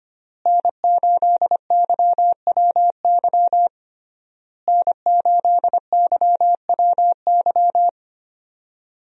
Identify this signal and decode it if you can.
Morse code, heard as a keyed tone, about 25 wpm, 700 Hz. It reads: N8YWY N8YWY
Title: USA Call Signs - 5 Characters - Repeated 3x - 25wpm